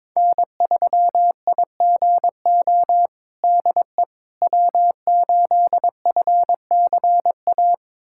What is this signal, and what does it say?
Morse code, heard as a keyed tone, about 22 wpm, 700 Hz. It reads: N3IGO DE W8FCA
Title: USA Callsign DE Callsign - 5 Characters - 22wpm